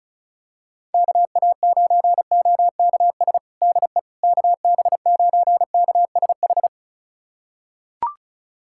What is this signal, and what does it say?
Morse code, heard as a keyed tone, about 34 wpm, 700 Hz. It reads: KA9OKS DE KB9KSH
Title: USA Callsign DE Callsign - 6 Characters - 35wpm